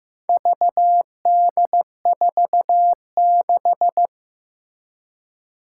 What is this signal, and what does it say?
Morse code, heard as a keyed tone, about 15 wpm, 700 Hz. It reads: VD46
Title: Four Challenging Letter-Number Combinations 15wpm